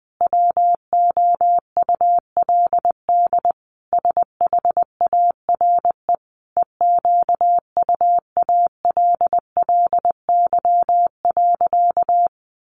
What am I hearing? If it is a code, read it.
Morse code, heard as a keyed tone, about 20 wpm, 700 Hz. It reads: WOULD SHARE EQUALLY.
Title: Sentences from Top 1400 Words 20wpm